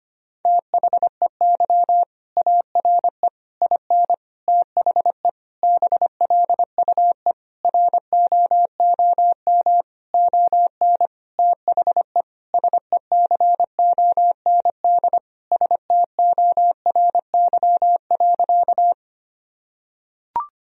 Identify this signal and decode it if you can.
Morse code, heard as a keyed tone, about 25 wpm, 700 Hz. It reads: THEY ARE IN THE BLUE ROOM ON THE SECOND STORY.
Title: Sentences from Top 1000 Words - Review Part 3 - 25wpm